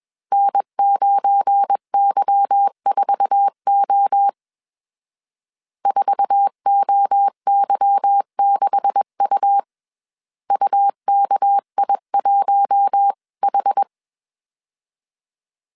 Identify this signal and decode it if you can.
Morse code, heard as a keyed tone, about 21 wpm, 800 Hz. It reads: N9Y4O 4OY6U UKI1H